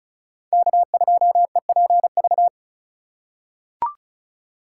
Morse code, heard as a keyed tone, about 36 wpm, 700 Hz. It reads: K2EPV
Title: USA Call Signs - 5 Characters - 3x Character Spacing - 35wpm